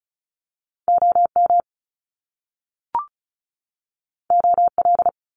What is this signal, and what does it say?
Morse code, heard as a keyed tone, about 34 wpm, 700 Hz. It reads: OM OL
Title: Top Two Letter Combinations in Words 35wpm